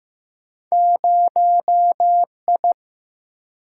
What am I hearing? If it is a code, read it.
Morse code, heard as a keyed tone, about 15 wpm, 700 Hz. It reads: 0I
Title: Single-Letter Single-Number 15wpm